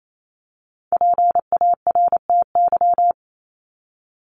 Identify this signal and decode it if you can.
Morse code, heard as a keyed tone, about 28 wpm, 700 Hz. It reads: PARTY
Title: ES:S Suffix - Top 100 Words with Base Words 28wpm